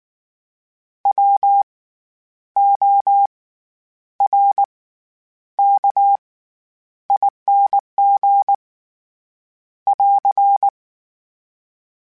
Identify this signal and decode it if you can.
Morse code, heard as a keyed tone, about 19 wpm, spaced as for 9 wpm, 800 Hz. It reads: WORKING <AR>
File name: Special INGs / Special INGs-mp3s / SlowToFaster ing @18